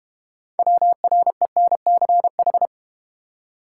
Morse code, heard as a keyed tone, about 32 wpm, 700 Hz. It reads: WRENCH